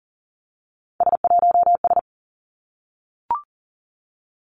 Morse code, heard as a keyed tone, about 38 wpm, 700 Hz. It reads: S1S
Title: Three Challenging Letter-Number Combinations 40wpm